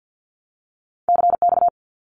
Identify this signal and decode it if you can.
Morse code, heard as a keyed tone, about 50 wpm, 700 Hz. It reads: CX
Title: Abbreviations 50wpm